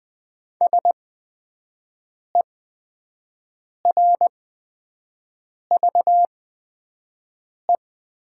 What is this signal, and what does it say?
Morse code, heard as a keyed tone, about 20 wpm, spaced as for 5 wpm, 700 Hz. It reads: SERVE
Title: 8x Character Spacing - Top 500 Words - 20wpm